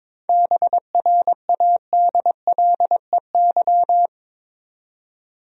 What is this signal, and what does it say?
Morse code, heard as a keyed tone, about 22 wpm, 700 Hz. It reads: BRADLEY
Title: USA Common Names - 22wpm